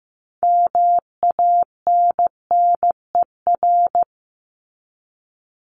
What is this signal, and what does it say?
Morse code, heard as a keyed tone, about 15 wpm, 700 Hz. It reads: MANNER
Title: Top 1501-1750 Words 15wpm